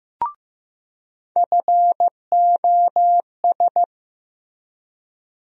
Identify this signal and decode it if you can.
Morse code, heard as a keyed tone, about 15 wpm, 700 Hz. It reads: FOS
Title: USA Call Sign Postfixes - 3 Letters 15wpm